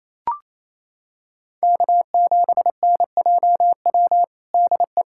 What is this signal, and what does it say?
Morse code, heard as a keyed tone, about 28 wpm, 700 Hz. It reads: K7NJW DE
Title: USA Callsign DE Callsign - 5 Characters - 28wpm